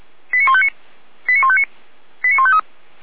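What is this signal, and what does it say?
Five-tone selective call (ZVEI), heard as quick runs of five tones, tones of about 70 ms. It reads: 89159 89159 89153
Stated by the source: ZVEI five-tone selective call received on SDR.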